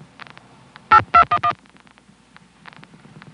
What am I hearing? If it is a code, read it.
DTMF, heard as telephone touch tones, tones of about 80 ms. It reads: 8211